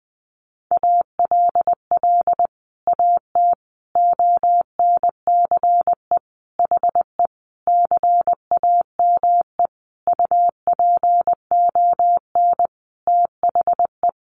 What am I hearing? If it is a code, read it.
Morse code, heard as a keyed tone, about 20 wpm, 700 Hz. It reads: ALL AT ONCE HE CAME UPON THE
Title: Sentences from Top 1100 Words 20wpm